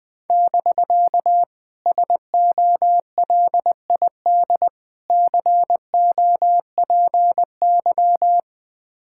Morse code, heard as a keyed tone, about 20 wpm, 700 Hz. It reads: <BK> SOLID COPY
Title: Speed-Racing Common QSO Phrases - 17wpm